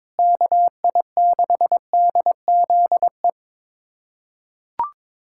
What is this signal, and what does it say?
Morse code, heard as a keyed tone, about 22 wpm, 700 Hz. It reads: KI6DZE
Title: USA Callsign DE Callsign 22wpm